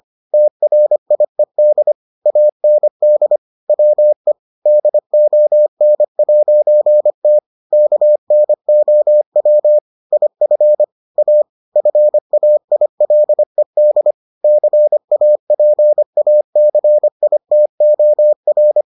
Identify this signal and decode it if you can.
Morse code, heard as a keyed tone, about 25 wpm, 600 Hz. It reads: TRIED AND WE DON'T KNOW IF A FAILED CAPACITOR